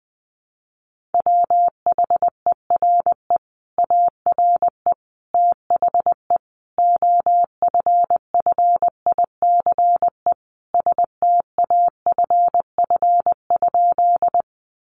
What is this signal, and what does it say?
Morse code, heard as a keyed tone, about 20 wpm, 700 Hz. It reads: WHERE ARE THE OFFICE STAFF?
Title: Sentences from Top 1200 Words 20wpm